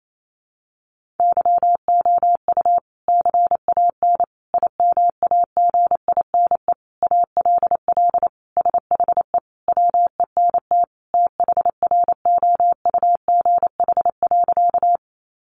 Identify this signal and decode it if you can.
Morse code, heard as a keyed tone, about 28 wpm, 700 Hz. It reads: YOU CAN IMAGINE ALL SHE WENT THROUGH.
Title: Sentences from Top 1400 Words 28wpm